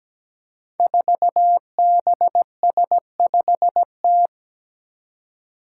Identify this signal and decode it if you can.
Morse code, heard as a keyed tone, about 17 wpm, 700 Hz. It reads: 4BS5T
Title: Five Challenging Letter-Number Combinations 17wpm